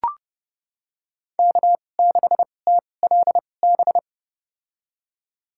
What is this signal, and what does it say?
Morse code, heard as a keyed tone, about 30 wpm, spaced as for 22 wpm, 700 Hz. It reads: K6TLB
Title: USA Call Signs - 2x Character Spacing - 30wpm